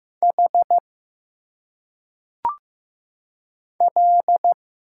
Morse code, heard as a keyed tone, about 15 wpm, 700 Hz. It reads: H L
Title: Single Letters - Phonetic Pronunciation 15wpm